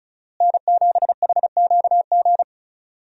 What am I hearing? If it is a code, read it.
Morse code, heard as a keyed tone, about 36 wpm, 700 Hz. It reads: N7HQG
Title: USA Call Signs - 5 Characters - Repeated 3x - 35wpm